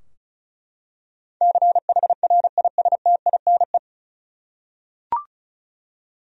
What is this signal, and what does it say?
Morse code, heard as a keyed tone, about 36 wpm, 700 Hz. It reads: CHRISTINE
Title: USA Common Names - 35wpm